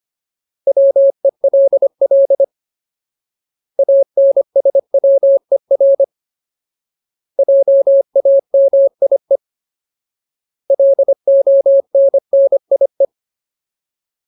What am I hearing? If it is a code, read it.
Morse code, heard as a keyed tone, about 25 wpm, 550 Hz. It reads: WELL ANSWER JAMIE LONNIE
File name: Ultimate Word List / 25WPM Ultimate